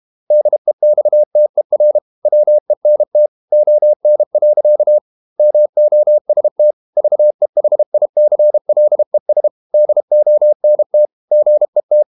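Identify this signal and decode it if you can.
Morse code, heard as a keyed tone, about 32 wpm, 600 Hz. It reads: DEXTER WENT ON. MOST VEHICLES DONT GET
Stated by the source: Morse code practice at 32 wpm from CW med Gnister, reading Ken Follett's novel Never in English.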